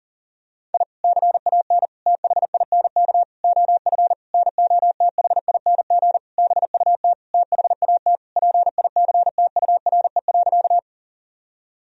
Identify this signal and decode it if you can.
Morse code, heard as a keyed tone, about 40 wpm, 700 Hz. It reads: I CAN THINK OF NOTHING BUT THAT PICTURE.
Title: Sentences from Top 1000 Words - Review Part 1 - 40wpm